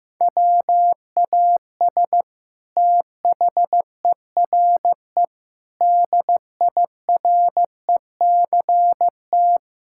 Morse code, heard as a keyed tone, about 15 wpm, 700 Hz. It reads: WAS THERE DIRECT
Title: Sentences from Top 1000 Words - Review Part 6 - 15wpm